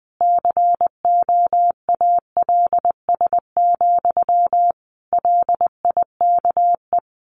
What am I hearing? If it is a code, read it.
Morse code, heard as a keyed tone, about 20 wpm, 700 Hz. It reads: COALS, LIKE
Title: A Sound Of Thunder 20wpm